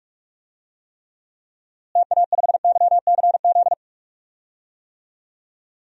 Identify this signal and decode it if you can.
Morse code, heard as a keyed tone, about 46 wpm, 700 Hz. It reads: TA5YCZ
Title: DXCC-100 Call Signs - 6 Characters - Repeated 3x - 45wpm